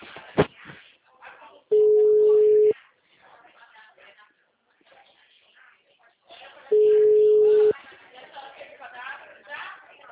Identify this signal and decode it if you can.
Ringback tone (Europe, 425 Hz), with speech.
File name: out- -2022- - - .8422